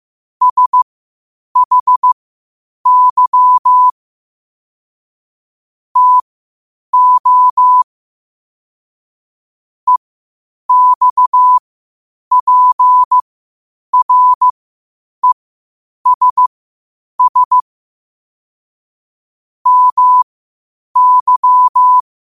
Morse code, heard as a keyed tone, about 15 wpm, spaced as for 8 wpm, 1000 Hz. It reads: SHY TO EXPRESS MY
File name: qotdfiles / 8wpm